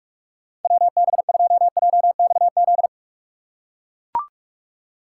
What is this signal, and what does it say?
Morse code, heard as a keyed tone, about 42 wpm, 700 Hz. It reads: WB2JXZ
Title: USA Call Signs - 6 Characters - 3x Character Spacing - 45wpm